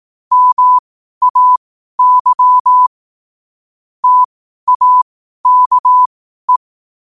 Morse code, heard as a keyed tone, about 18 wpm, spaced as for 12 wpm, 1000 Hz. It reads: MAY TAKE